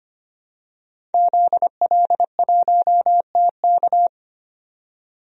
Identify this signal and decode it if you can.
Morse code, heard as a keyed tone, about 25 wpm, 700 Hz. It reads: ZL1TK
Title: DXCC-100 Call Signs - 5 Characters - 25wpm